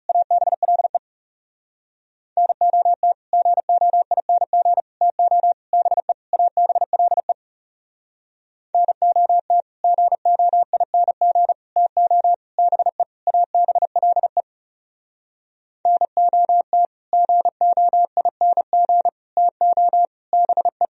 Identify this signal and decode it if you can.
Morse code, heard as a keyed tone, about 34 wpm, 700 Hz. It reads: ABLE NOT GOING TO BE ABLE NOT GOING TO BE ABLE NOT GOING TO BE
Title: Speed-Racing - Sets of 5 Words 30wpm